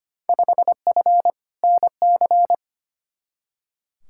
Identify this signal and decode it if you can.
Morse code, heard as a keyed tone, about 25 wpm, 700 Hz. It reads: 5F NC
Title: Field Day Exchanges Encore - 25wpm